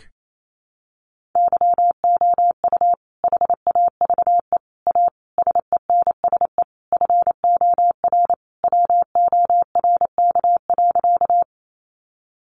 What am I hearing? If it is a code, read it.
Morse code, heard as a keyed tone, about 28 wpm, 700 Hz. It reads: YOU HAVE A SENSE FOR WORK.
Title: Sentences from Top 700 Words 28wpm